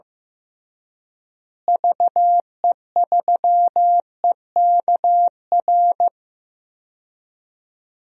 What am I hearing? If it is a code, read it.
Morse code, heard as a keyed tone, about 15 wpm, 700 Hz. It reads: VE3EKR
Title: DXCC-100 Call Signs - 6 Characters - Repeated 3x - 15wpm